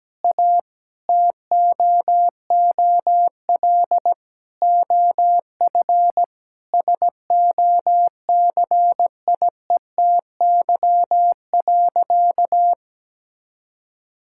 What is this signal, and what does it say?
Morse code, heard as a keyed tone, about 17 wpm, 700 Hz. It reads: A TOOL OF SOCIETY.
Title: Sentences from Top 1000 Words - Review Part 6 - 17wpm